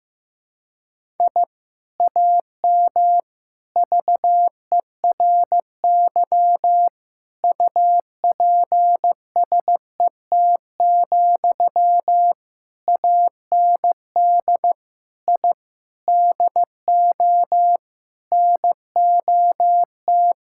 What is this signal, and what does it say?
Morse code, heard as a keyed tone, about 15 wpm, 700 Hz. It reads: I AM VERY UPSET, AND I DO NOT
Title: Sentences from Top 1000 Words - Review Part 5 - 15wpm